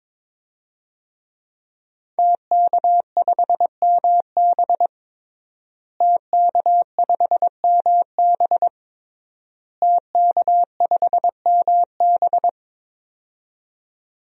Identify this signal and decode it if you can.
Morse code, heard as a keyed tone, about 22 wpm, 700 Hz. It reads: TK5MB TK5MB TK5MB
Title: DXCC-100 Call Signs - 5 Characters - Repeated 3x - 22wpm